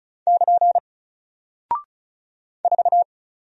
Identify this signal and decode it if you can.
Morse code, heard as a keyed tone, about 36 wpm, 700 Hz. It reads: <KN> 4
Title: Single Character (Letter:Number:Punctuation:Prosign) Alt 35wpm